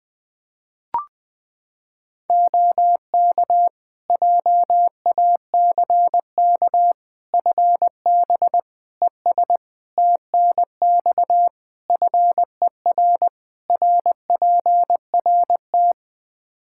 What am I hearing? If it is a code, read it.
Morse code, heard as a keyed tone, about 20 wpm, 700 Hz. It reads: OK JACK FB ES TNX FER RPRT
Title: Practice QSO - Exchange 2 - Info - 20wpm